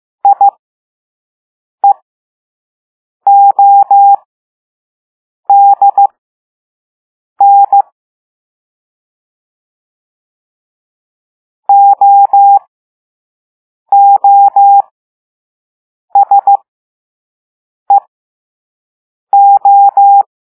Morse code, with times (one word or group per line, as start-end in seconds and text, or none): 0.2-7.8 IEODN
11.7-20.2 OOSEO